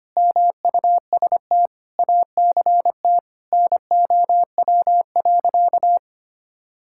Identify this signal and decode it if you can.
Morse code, heard as a keyed tone, about 25 wpm, 700 Hz. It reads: MUST ACT NOW.
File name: practice / 25wpm / Sentences with 9-Word Phrases - 25wpm